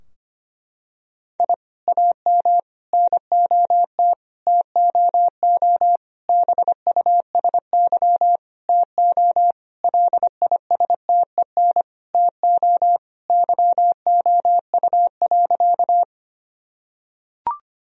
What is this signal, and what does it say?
Morse code, heard as a keyed tone, about 25 wpm, 700 Hz. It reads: I AM NOT TOO BUSY TO LISTEN TO YOU.
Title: Sentences from Top 1000 Words - Review Part 3 - 25wpm